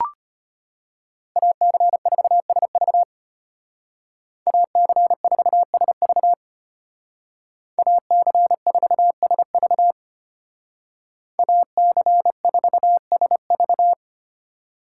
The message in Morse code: AC4SV AC4SV AC4SV AC4SV

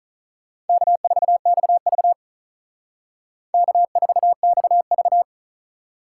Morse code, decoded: K4XV K4XV

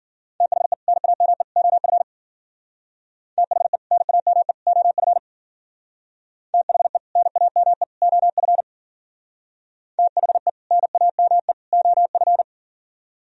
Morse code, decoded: THE NAME OF THE NAME OF THE NAME OF THE NAME OF

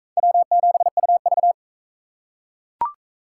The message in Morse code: W7UV

